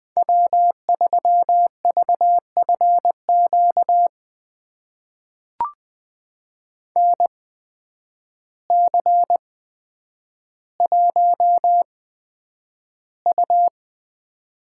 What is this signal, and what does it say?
Morse code, heard as a keyed tone, about 20 wpm, spaced as for 9 wpm, 700 Hz. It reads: W3VFQ NC1U
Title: USA Call Signs - 8x Character Spacing - 20wpm